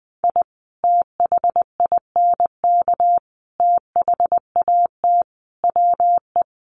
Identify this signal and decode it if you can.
Morse code, heard as a keyed tone, about 20 wpm, 700 Hz. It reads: I THINK THAT WE
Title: Sets of 5 Words Encore2 20wpm